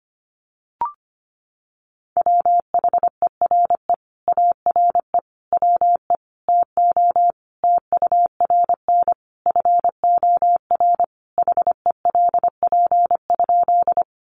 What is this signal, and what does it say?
Morse code, heard as a keyed tone, about 25 wpm, 700 Hz. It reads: WHERE ARE WE TO TURN FOR HELP?
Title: Sentences from Top 200 Words 25wpm